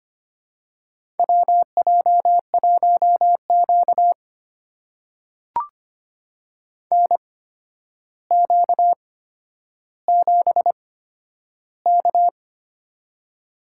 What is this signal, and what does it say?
Morse code, heard as a keyed tone, about 25 wpm, spaced as for 11 wpm, 700 Hz. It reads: WJ1Q NQ7K
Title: USA Call Signs - 4 Characters - 8x Character Spacing - 25wpm